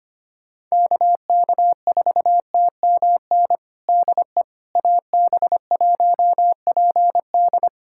Morse code, heard as a keyed tone, about 25 wpm, 700 Hz. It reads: KK4TMN DE AB1PD
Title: USA Callsign DE Callsign 25wpm